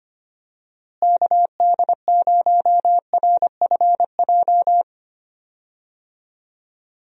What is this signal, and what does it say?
Morse code, heard as a keyed tone, about 25 wpm, 700 Hz. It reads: KD0RFJ
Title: USA Call Signs - 6 Characters - Repeated 3x - 25wpm